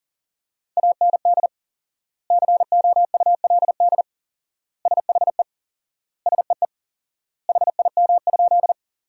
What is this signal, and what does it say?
Morse code, heard as a keyed tone, about 40 wpm, 700 Hz. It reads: AND COULD SHE SEE HIM?
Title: 4x Word-Spacing - Sentences from Top 100 Words - 40wpm